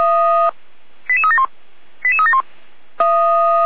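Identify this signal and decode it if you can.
Five-tone selective call (ZVEI), heard as quick runs of five tones, tones of about 70 ms. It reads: 88371 88371
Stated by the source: ZVEI five-tone selective call received by SDR, address 88325.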